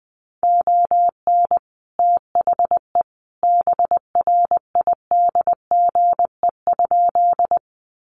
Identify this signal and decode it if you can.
Morse code, heard as a keyed tone, about 20 wpm, 700 Hz. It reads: ON THE BRIDGE?
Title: Sentences from Top 1200 Words 20wpm